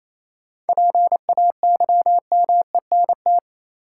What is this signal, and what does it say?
Morse code, heard as a keyed tone, about 28 wpm, 700 Hz. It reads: PAYMENT
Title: Sentences from Top 1200 Words - New Word Drill Alt - 28wpm